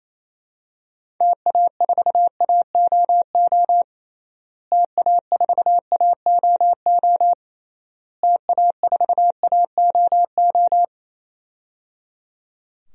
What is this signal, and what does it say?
Morse code, heard as a keyed tone, about 28 wpm, 700 Hz. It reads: TA4AOO TA4AOO TA4AOO
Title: DXCC-100 Call Signs - 6 Characters - Repeated 3x - 28wpm